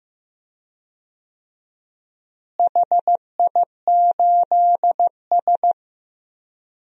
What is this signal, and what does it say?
Morse code, heard as a keyed tone, about 15 wpm, 700 Hz. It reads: HI8S
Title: DXCC-100 Call Signs - 4 Characters - Repeated 3x - 15wpm